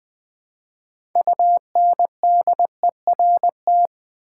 Morse code, heard as a keyed tone, about 20 wpm, 700 Hz. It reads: UNDERT
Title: Build A Word - 3 Letters At A Time - 9-Letter Words - 20wpm